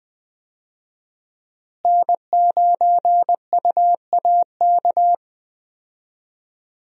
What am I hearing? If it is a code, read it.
Morse code, heard as a keyed tone, about 20 wpm, 700 Hz. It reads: N9UAK
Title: USA Call Signs - 5 Characters - Repeated 2x - 20wpm